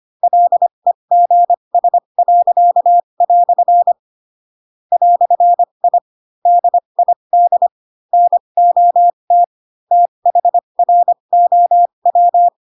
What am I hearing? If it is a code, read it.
Morse code, heard as a keyed tone, about 25 wpm, spaced as for 22 wpm, 700 Hz. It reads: LEGS." "I DID NOT THROW
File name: Pinocchio / Pinocchio ch2 18 25 ik1hge